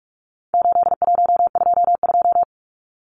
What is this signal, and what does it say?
Morse code, heard as a keyed tone, about 44 wpm, 700 Hz. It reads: 8122
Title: Numbers - 4-Digit 45wpm